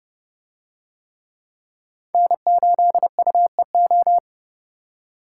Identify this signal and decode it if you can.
Morse code, heard as a keyed tone, about 31 wpm, 700 Hz. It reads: N8UEO